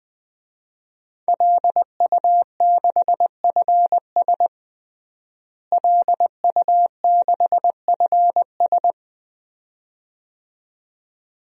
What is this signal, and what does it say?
Morse code, heard as a keyed tone, about 20 wpm, 700 Hz. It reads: LU6FS LU6FS
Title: DXCC-100 Call Signs - 5 Characters - Repeated 2x - 20wpm